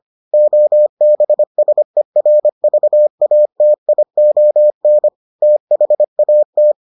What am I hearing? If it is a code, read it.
Morse code, heard as a keyed tone, about 25 wpm, 600 Hz. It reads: OBSERVATION THAT